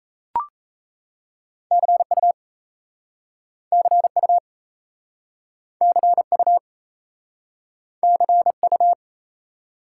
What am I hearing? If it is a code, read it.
Morse code, heard as a keyed tone, about 37 wpm, 700 Hz. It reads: CU CU CU CU